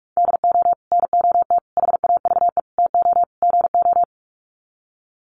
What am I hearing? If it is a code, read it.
Morse code, heard as a keyed tone, about 44 wpm, 700 Hz. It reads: DO NOT HAVE TO GO
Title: Sets of 5 Words 45wpm